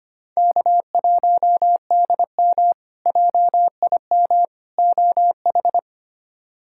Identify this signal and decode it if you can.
Morse code, heard as a keyed tone, about 25 wpm, 700 Hz. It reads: K1DM JIM OH
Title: K1USN SST QSO Practice - 25wpm